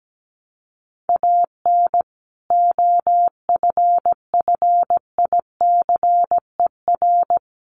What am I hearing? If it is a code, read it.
Morse code, heard as a keyed tone, about 17 wpm, 700 Hz. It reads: AN OFFICER
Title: Sentences from Top 1200 Words 17wpm